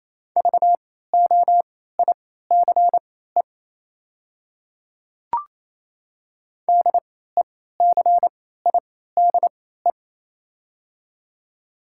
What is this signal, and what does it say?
Morse code, heard as a keyed tone, about 27 wpm, spaced as for 16 wpm, 700 Hz. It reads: VOICE DECIDE
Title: 3x Character Spacing - Top 500 Words - 28wpm